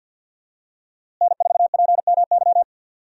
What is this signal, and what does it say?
Morse code, heard as a keyed tone, about 53 wpm, 700 Hz. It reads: N4PMY
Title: USA Call Signs - 5 Characters - Repeated 2x - 50wpm